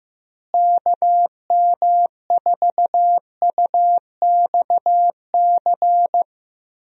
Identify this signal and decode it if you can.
Morse code, heard as a keyed tone, about 15 wpm, 700 Hz. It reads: KM4UXC
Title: USA Call Signs - 6 Characters - Repeated 2x - 15wpm